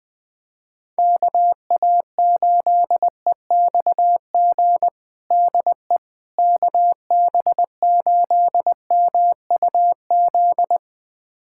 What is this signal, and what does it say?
Morse code, heard as a keyed tone, about 20 wpm, 700 Hz. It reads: KA8EXG DE KB8MUZ
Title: USA Callsign DE Callsign 20wpm